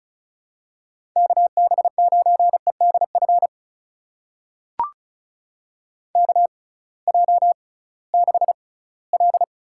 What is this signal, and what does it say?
Morse code, heard as a keyed tone, about 36 wpm, spaced as for 21 wpm, 700 Hz. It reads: KB9EDF KJ6L
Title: USA Call Signs - 6 Characters - 6x Character Spacing - 35wpm